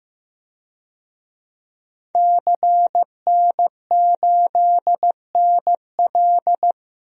Morse code, heard as a keyed tone, about 15 wpm, 700 Hz. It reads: CN8NL
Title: DXCC-100 Call Signs - 5 Characters - 15wpm